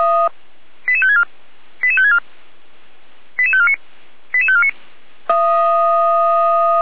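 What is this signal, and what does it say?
Five-tone selective call (ZVEI), heard as quick runs of five tones, tones of about 70 ms. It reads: 88564 88564 88549 88549
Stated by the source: ZVEI five-tone call 88549 received by SDR at Feuerwehr Orlamünde (f85360).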